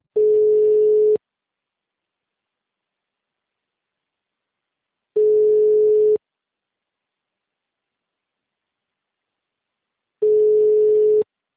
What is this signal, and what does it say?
Ringback tone (Europe, 425 Hz).